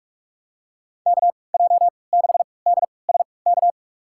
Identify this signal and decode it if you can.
Morse code, heard as a keyed tone, about 46 wpm, spaced as for 26 wpm, 700 Hz. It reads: KJ6DSK
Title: USA Call Signs - 3x Character Spacing - 45wpm